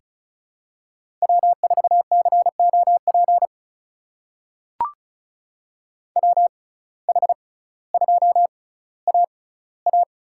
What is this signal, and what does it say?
Morse code, heard as a keyed tone, about 36 wpm, spaced as for 18 wpm, 700 Hz. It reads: W4COP WH2AA
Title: USA Call Signs - 6x Character Spacing - 35wpm